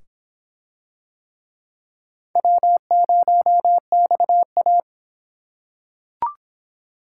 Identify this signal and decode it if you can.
Morse code, heard as a keyed tone, about 22 wpm, 700 Hz. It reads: W0XA